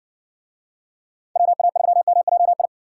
Morse code, heard as a keyed tone, about 60 wpm, 700 Hz. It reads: WA3MJI